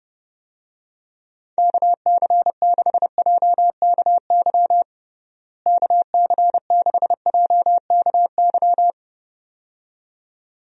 Morse code, heard as a keyed tone, about 30 wpm, 700 Hz. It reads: KC6JKY KC6JKY
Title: USA Call Signs - 6 Characters - Repeated 2x - 30wpm